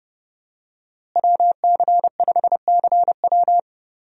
Morse code, heard as a keyed tone, about 30 wpm, 700 Hz. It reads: WC5CW
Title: USA Call Signs - 5 Characters - Repeated 3x - 30wpm